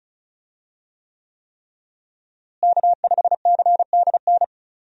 Morse code, heard as a keyed tone, about 34 wpm, 700 Hz. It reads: K5CDN
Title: USA Call Signs - 5 Characters - Repeated 3x - 35wpm